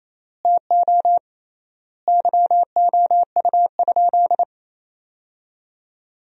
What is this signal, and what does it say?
Morse code, heard as a keyed tone, about 28 wpm, 700 Hz. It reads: TO YOU?
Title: 3x Word-Spacing - Sentences from Top 100 Words - 28wpm